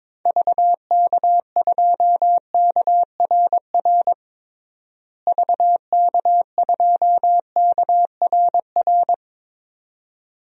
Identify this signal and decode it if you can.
Morse code, heard as a keyed tone, about 22 wpm, 700 Hz. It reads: VK2KRR VK2KRR
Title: DXCC-100 Call Signs - 6 Characters - Repeated 2x - 22wpm